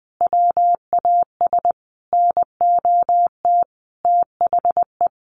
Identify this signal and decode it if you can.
Morse code, heard as a keyed tone, about 20 wpm, 700 Hz. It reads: WAS NOT THE
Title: Sentences from Top 1400 Words 20wpm